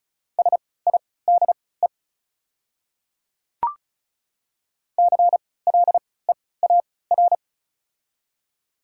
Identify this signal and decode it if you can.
Morse code, heard as a keyed tone, about 36 wpm, spaced as for 20 wpm, 690 Hz. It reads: SIDE CLEAR
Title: 3x Character Spacing - Top 500 Words - 35wpm